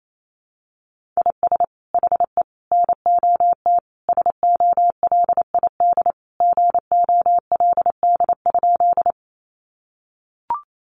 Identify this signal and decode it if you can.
Morse code, heard as a keyed tone, about 28 wpm, 700 Hz. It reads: IS HE NOT SOLID GOLD?
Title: Sentences from Top 1500 Words 28wpm